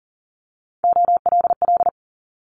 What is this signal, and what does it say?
Morse code, heard as a keyed tone, about 38 wpm, 700 Hz. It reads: OLL
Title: Multi-letter Phonemes 40wpm